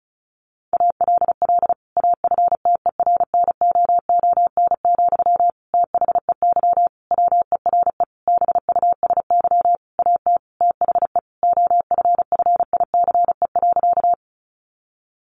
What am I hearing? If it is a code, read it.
Morse code, heard as a keyed tone, about 34 wpm, 700 Hz. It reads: ALL AFTERNOON, THEY WERE BUSY AT THE OFFICE.
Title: Sentences from Top 1500 Words 35wpm